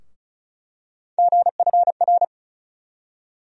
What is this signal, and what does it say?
Morse code, heard as a keyed tone, about 34 wpm, 700 Hz. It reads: GFR
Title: USA Call Sign Postfixes - 3 Letters 35wpm